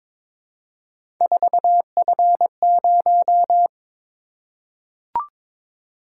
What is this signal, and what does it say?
Morse code, heard as a keyed tone, about 22 wpm, 700 Hz. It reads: 4F0
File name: practice / 22wpm / Three Letter-Number Combinations - 22wpm